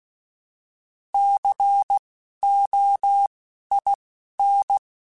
Morse code, heard as a keyed tone, about 16 wpm, spaced as for 11 wpm, 800 Hz. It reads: COIN
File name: Alphabetical / Alphabetical mp3s / Cs @10